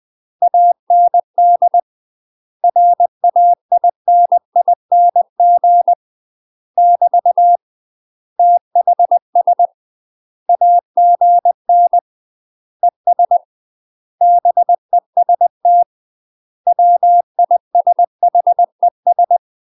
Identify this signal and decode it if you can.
Morse code, heard as a keyed tone, about 20 wpm, 700 Hz. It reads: AND RAINING <BT> THS AGN ES BEST WISHES
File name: UK MASTER QSOs TRAINING / 20WPM / UK MASTER QSOs 0020